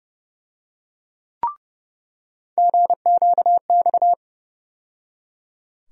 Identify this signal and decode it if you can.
Morse code, heard as a keyed tone, about 30 wpm, 700 Hz. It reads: GQX
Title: USA Call Sign Postfixes - 3 Letters 30wpm